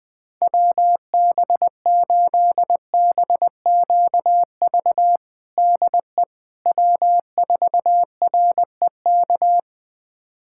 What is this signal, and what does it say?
Morse code, heard as a keyed tone, about 20 wpm, 700 Hz. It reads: WB8BQV DE W4REK
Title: USA Callsign DE Callsign 20wpm